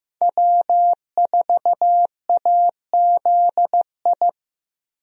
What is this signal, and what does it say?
Morse code, heard as a keyed tone, about 15 wpm, 700 Hz. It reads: W4AZI
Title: USA Call Signs - 5 Characters - Repeated 3x - 15wpm